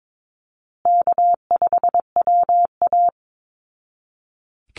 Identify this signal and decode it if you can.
Morse code, heard as a keyed tone, about 22 wpm, 700 Hz. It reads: K5WA
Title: Easy Call Signs Repeated 3x - 22wpm